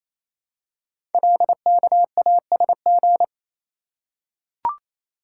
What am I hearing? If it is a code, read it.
Morse code, heard as a keyed tone, about 27 wpm, 700 Hz. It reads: LKASG
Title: Code Groups - 5 Letters - 28wpm